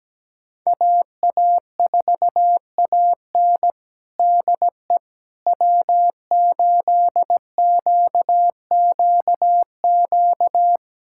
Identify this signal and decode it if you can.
Morse code, heard as a keyed tone, about 17 wpm, 700 Hz. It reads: AA4AN DE W8QQQ
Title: USA Callsign DE Callsign 17wpm